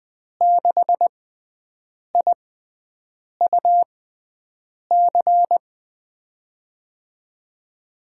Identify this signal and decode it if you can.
Morse code, heard as a keyed tone, about 20 wpm, spaced as for 7 wpm, 700 Hz. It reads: 6IUC
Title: USA Call Signs - 6 Characters - 6x Character Spacing - 20wpm